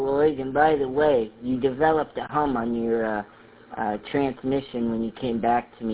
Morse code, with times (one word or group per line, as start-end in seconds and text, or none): none